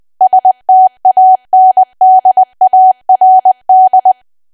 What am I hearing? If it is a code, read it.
Morse code, heard as a keyed tone, about 20 wpm, 750 Hz. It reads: STANDARD
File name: cwa / W1AW-Level3 / 20WPM